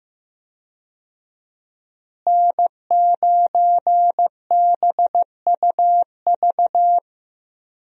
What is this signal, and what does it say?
Morse code, heard as a keyed tone, about 15 wpm, 700 Hz. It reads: N9BUV
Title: USA Call Signs - 5 Characters - Repeated 3x - 15wpm